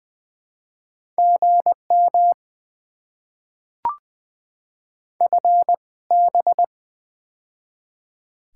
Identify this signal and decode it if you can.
Morse code, heard as a keyed tone, about 20 wpm, spaced as for 17 wpm, 690 Hz. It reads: GM FB
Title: Top 20 QSO Elements - 2x Character Spacing - 20wpm